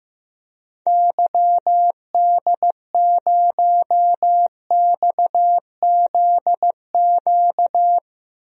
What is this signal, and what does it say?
Morse code, heard as a keyed tone, about 15 wpm, 700 Hz. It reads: YD0XZQ
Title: DXCC-100 Call Signs - 6 Characters - 15wpm